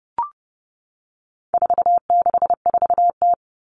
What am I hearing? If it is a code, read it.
Morse code, heard as a keyed tone, about 29 wpm, 700 Hz. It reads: 464T
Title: All Cut Numbers - 4-Digit 30wpm